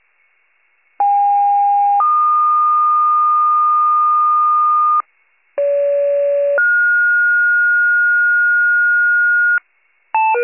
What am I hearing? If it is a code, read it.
Two-tone page, heard as a one-second tone, then a three-second tone, about 810 then 1220 Hz, about 570 then 1430 Hz, with noise. It is heard from 1.0 to 9.6 s.